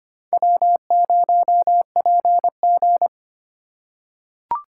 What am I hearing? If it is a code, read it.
Morse code, heard as a keyed tone, about 25 wpm, 700 Hz. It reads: W0PG
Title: USA Call Signs - 3x Character Spacing - 25wpm